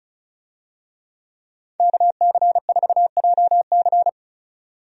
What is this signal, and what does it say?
Morse code, heard as a keyed tone, about 34 wpm, 700 Hz. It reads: KC4JC
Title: USA Call Signs - 5 Characters - 35wpm